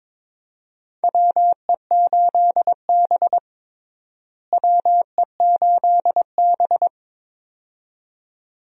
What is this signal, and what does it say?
Morse code, heard as a keyed tone, about 21 wpm, 700 Hz. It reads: WE8B WE8B
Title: USA Call Signs - 4 Characters - Repeated 2x - 22wpm